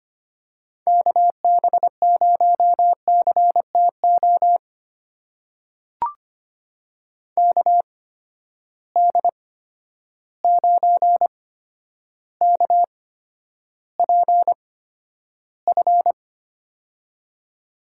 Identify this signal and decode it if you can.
Morse code, heard as a keyed tone, about 25 wpm, spaced as for 11 wpm, 700 Hz. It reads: KB0CTO KD9KPF
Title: USA Call Signs - 6 Characters - 8x Character Spacing - 25wpm